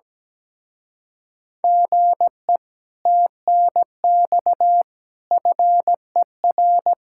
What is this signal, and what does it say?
Morse code, heard as a keyed tone, about 17 wpm, 700 Hz. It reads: GE TNX FER
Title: Speed-Racing Common QSO Phrases Encore - 17wpm